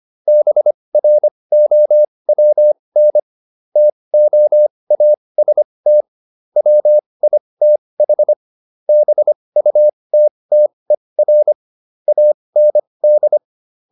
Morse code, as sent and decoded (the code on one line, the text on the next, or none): -... .-. --- .-- -. / - --- .- ... - / .-- .. - .... / -... ..- - - . .-. / .- -. -..
BROWN TOAST WITH BUTTER AND